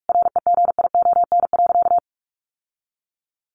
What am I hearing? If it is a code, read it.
Morse code, heard as a keyed tone, about 44 wpm, 700 Hz. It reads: REGION.
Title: Sentences from Top 1500 Words 45wpm